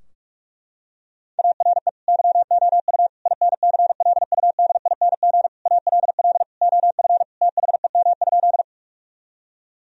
Morse code, heard as a keyed tone, about 46 wpm, 700 Hz. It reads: ARE YOU INCLUDING ALL OF THEM?
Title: Sentences from Top 1000 Words - Review Part 4 - 45wpm